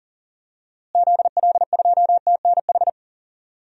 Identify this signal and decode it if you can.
Morse code, heard as a keyed tone, about 40 wpm, 700 Hz. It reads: ZL2TNH
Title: DXCC-100 Call Signs - 6 Characters - Repeated 2x - 40wpm